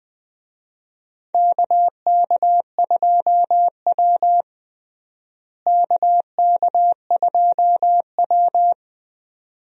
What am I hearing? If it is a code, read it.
Morse code, heard as a keyed tone, about 20 wpm, 700 Hz. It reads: KK2W KK2W
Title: USA Call Signs - 4 Characters - Repeated 2x - 20wpm